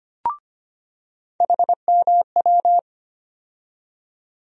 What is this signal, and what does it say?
Morse code, heard as a keyed tone, about 25 wpm, 700 Hz. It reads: HMW
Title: USA Call Sign Postfixes - 3 Letters 25wpm